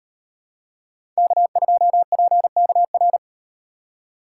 Morse code, heard as a keyed tone, about 38 wpm, 700 Hz. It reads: K2PKR